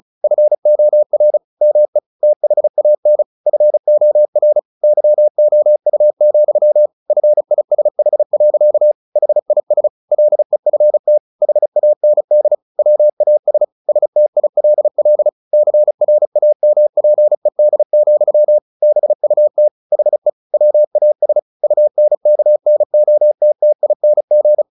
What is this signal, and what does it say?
Morse code, heard as a keyed tone, about 34 wpm, 600 Hz. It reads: FOR ME THAN FOR YOU, FISH. HIS LEFT HAND WAS STILL CRAMPED, BUT HE WAS UNKNOTTING
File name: the old man and the sea / the old man and the sea 1 35wpm